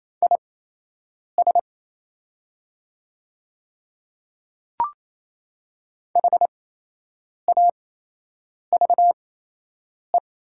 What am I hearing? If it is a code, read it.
Morse code, heard as a keyed tone, about 28 wpm, spaced as for 8 wpm, 700 Hz. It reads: IS HAVE
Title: 8x Character Spacing - Top 100 Words - 28wpm